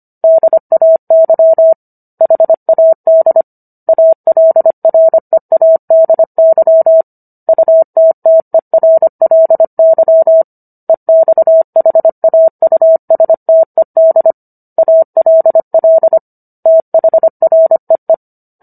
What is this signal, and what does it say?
Morse code, heard as a keyed tone, about 25 wpm, 650 Hz. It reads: DAY HAD ALREADY UTTERLY EXHAUSTED ALL THREE
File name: Book One / 16-Chapter 16 - 25 WPM